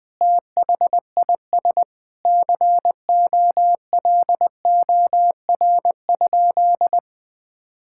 Morse code, heard as a keyed tone, about 20 wpm, 700 Hz. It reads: THIS COLOR?
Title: Sentences from Top 1000 Words - Review Part 6 - 20wpm